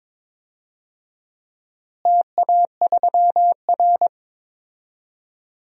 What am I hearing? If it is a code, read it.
Morse code, heard as a keyed tone, about 22 wpm, 700 Hz. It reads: TA3R
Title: DXCC-100 Call Signs - 4 Characters - Repeated 3x - 22wpm